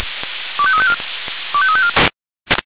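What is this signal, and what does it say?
Five-tone selective call (ZVEI), heard as quick runs of five tones, tones of about 70 ms. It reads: 25354 25354
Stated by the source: Five-tone ZVEI selective call received by SDR.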